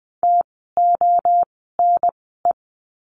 Morse code, heard as a keyed tone, about 20 wpm, spaced as for 14 wpm, 700 Hz. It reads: TONE
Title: 068 Lesson 12f - Letter d - Words 20wpm